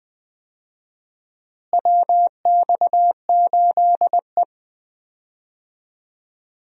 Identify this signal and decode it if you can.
Morse code, heard as a keyed tone, about 20 wpm, 700 Hz. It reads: WX8E
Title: USA Call Signs - 4 Characters - Repeated 2x - 20wpm